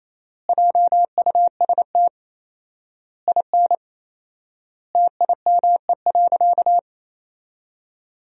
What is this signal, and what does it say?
Morse code, heard as a keyed tone, about 28 wpm, 700 Hz. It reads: JUST IN TIME.
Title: 4x Word-Spacing - Sentences from Top 300 Words - 28wpm